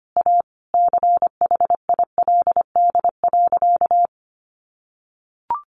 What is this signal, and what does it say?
Morse code, heard as a keyed tone, about 25 wpm, 700 Hz. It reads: A CHILD.
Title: Sentences from Top 1500 Words 25wpm